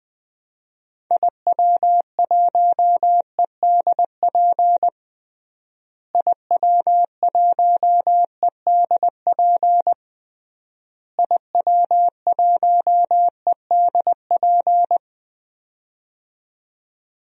Morse code, heard as a keyed tone, about 20 wpm, 700 Hz. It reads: IW1EDP IW1EDP IW1EDP
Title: DXCC-100 Call Signs - 6 Characters - Repeated 3x - 20wpm